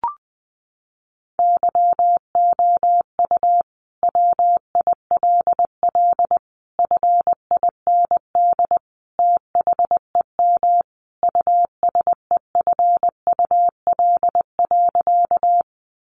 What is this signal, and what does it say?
Morse code, heard as a keyed tone, about 20 wpm, 700 Hz. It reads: YOU WILL FIND THEM USEFUL.
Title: Sentences from Top 700 Words 20wpm